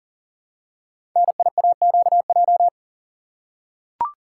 Morse code, heard as a keyed tone, about 41 wpm, 690 Hz. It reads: NIAQJ
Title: Code Groups - 5 Letters - 40wpm